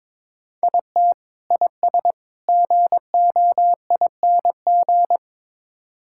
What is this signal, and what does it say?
Morse code, heard as a keyed tone, about 22 wpm, 700 Hz. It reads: IT IS GOING
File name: practice / 22wpm / Build A Phrase - 5-Word Phrases - 22wpm